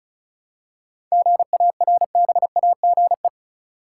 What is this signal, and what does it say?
Morse code, heard as a keyed tone, about 34 wpm, 700 Hz. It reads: GARBAGE
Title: Seven Letter Words - 35wpm